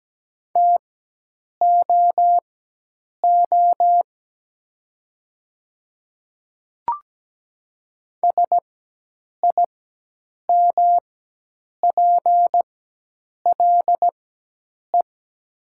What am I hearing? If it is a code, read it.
Morse code, heard as a keyed tone, about 17 wpm, spaced as for 8 wpm, 700 Hz. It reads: TOO SIMPLE
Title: 4x Character Spacing - Top 500 Words - 17wpm